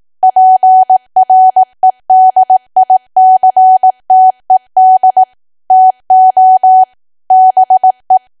Morse code, heard as a keyed tone, about 18 wpm, 750 Hz. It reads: PREDICTED TO BE
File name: cwa / W1AW-Level2 / 18WPM